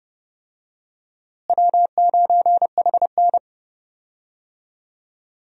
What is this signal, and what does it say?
Morse code, heard as a keyed tone, about 29 wpm, 700 Hz. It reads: W9HN
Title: USA Call Signs - 4 Characters - Repeated 3x - 30wpm